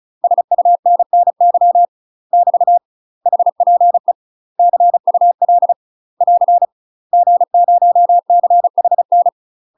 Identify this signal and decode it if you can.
Morse code, heard as a keyed tone, about 36 wpm, 700 Hz. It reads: SUNNY <BT> HPE CUL <AR> G0CHN
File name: UK MASTER QSOs TRAINING / 35WPM / UK MASTER QSOs 0036